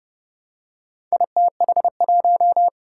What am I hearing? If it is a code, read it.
Morse code, heard as a keyed tone, about 30 wpm, 700 Hz. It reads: ITH1
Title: Four Challenging Letter-Number Combinations 30wpm